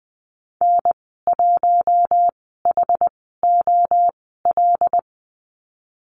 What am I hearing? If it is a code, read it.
Morse code, heard as a keyed tone, about 20 wpm, spaced as for 14 wpm, 700 Hz. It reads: N1HOL
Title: 088 Lesson 15g - Number 5 - Call Signs 20wpm